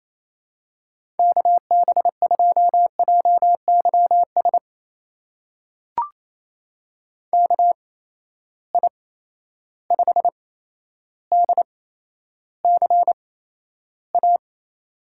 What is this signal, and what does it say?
Morse code, heard as a keyed tone, about 28 wpm, spaced as for 12 wpm, 700 Hz. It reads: KB2JYS KI5DCA
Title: USA Call Signs - 6 Characters - 8x Character Spacing - 28wpm